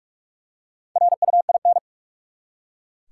Morse code, heard as a keyed tone, about 43 wpm, 700 Hz. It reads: RUIN
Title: Four Letter Words - 45wpm